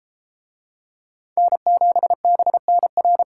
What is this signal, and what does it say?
Morse code, heard as a keyed tone, about 30 wpm, 700 Hz. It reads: N7BNR